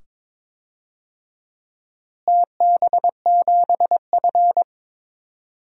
Morse code, heard as a keyed tone, about 22 wpm, 700 Hz. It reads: TB7F
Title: DXCC-100 Call Signs - 4 Characters - Repeated 2x - 22wpm